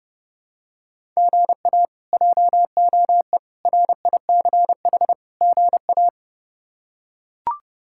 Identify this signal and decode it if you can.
Morse code, heard as a keyed tone, about 32 wpm, 700 Hz. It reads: GA JOE RICH GA
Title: K1USN SST QSO Practice - 30wpm